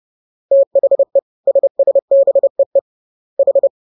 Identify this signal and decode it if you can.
Morse code, heard as a keyed tone, about 30 wpm, 550 Hz. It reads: THE SSBEE H
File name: jokes / joke 26 30 0000